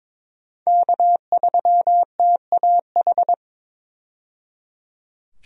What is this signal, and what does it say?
Morse code, heard as a keyed tone, about 22 wpm, 700 Hz. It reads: K3TAH
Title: USA Call Signs - 5 Characters - 22wpm